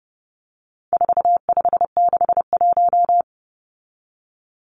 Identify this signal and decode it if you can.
Morse code, heard as a keyed tone, about 30 wpm, 700 Hz. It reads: 4561
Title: Numbers - 4-Digit 30wpm